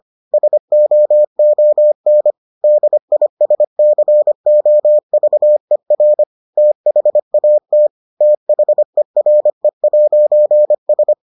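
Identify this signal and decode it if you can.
Morse code, heard as a keyed tone, about 25 wpm, 600 Hz. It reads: SOON DISCOVER THAT THERE'S
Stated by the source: Morse edition of the Foundations of Amateur Radio episodes.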